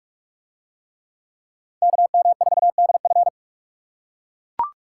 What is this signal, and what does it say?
Morse code, heard as a keyed tone, about 42 wpm, 700 Hz. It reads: KM4DF